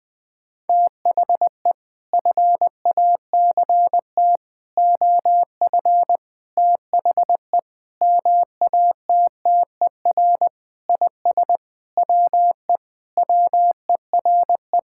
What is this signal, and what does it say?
Morse code, heard as a keyed tone, about 20 wpm, 700 Hz. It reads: THE FACT OF THE MATTER IS WE WERE
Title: Sentences with 5-Word Phrases - 20wpm